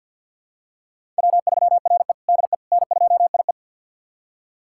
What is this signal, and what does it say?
Morse code, heard as a keyed tone, about 49 wpm, 700 Hz. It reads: W3RE DE N2IE
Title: USA Callsign DE Callsign - 4 Characters - 50wpm